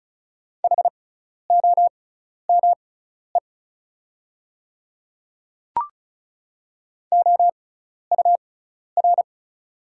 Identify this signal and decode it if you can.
Morse code, heard as a keyed tone, about 36 wpm, spaced as for 12 wpm, 700 Hz. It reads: HOME OUR
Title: 6x Character Spacing - Top 200 Words - 35wpm